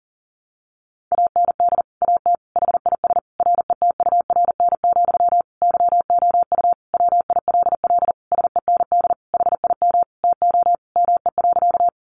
Morse code, heard as a keyed tone, about 37 wpm, 700 Hz. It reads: AND AT HIS RETURN, YOU WILL SEND HIM TO ME.
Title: Sentences from Top 1500 Words 40wpm